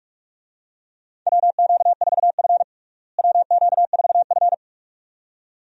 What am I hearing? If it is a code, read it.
Morse code, heard as a keyed tone, about 46 wpm, 700 Hz. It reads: WQ4F WQ4F
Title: USA Call Signs - 4 Characters - Repeated 2x - 45wpm